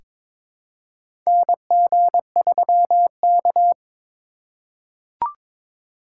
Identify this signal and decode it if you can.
Morse code, heard as a keyed tone, about 22 wpm, 700 Hz. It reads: NG3K
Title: USA Call Signs - 4 Characters - 3x Character Spacing - 22wpm